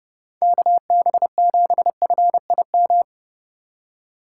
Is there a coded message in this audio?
Yes: Morse code from 0.4 to 3.0 s, heard as a keyed tone, about 32 wpm, 700 Hz. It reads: KB7FIM